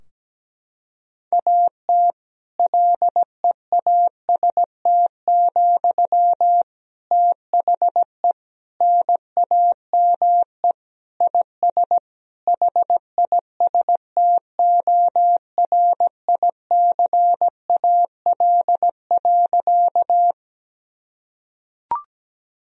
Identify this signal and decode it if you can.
Morse code, heard as a keyed tone, about 17 wpm, 700 Hz. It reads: AT LEAST, THE NAME IS HISTORICAL.
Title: Sentences from Top 1000 Words - Review Part 4 - 17wpm